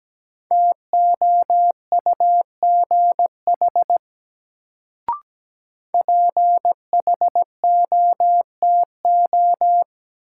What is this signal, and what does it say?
Morse code, heard as a keyed tone, about 17 wpm, 700 Hz. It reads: TOUGH PHOTO
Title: Sentences from Top 1200 Words - New Word Drill Alt - 17wpm